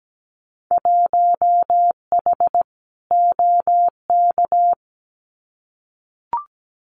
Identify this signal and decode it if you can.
Morse code, heard as a keyed tone, about 17 wpm, 700 Hz. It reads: 1H OK
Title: Winter Field Day Exchanges 17wpm